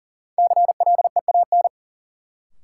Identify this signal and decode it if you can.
Morse code, heard as a keyed tone, about 38 wpm, 700 Hz. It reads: CLEAN
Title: Sentences from Top 1100 Words - New Word Drill - 40wpm